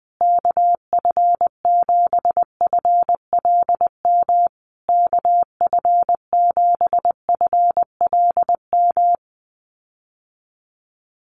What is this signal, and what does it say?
Morse code, heard as a keyed tone, about 20 wpm, 700 Hz. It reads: KF7FLM KF7FLM
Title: USA Call Signs Repeated 3x 20wpm